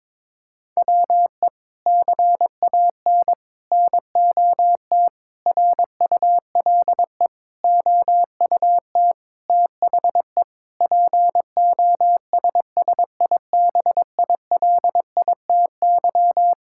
Morse code, heard as a keyed tone, about 22 wpm, 700 Hz. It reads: WE CAN NOT RULE OUT THE POSSIBILITY
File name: practice / 22wpm / Sets of 8 Words - 22wpm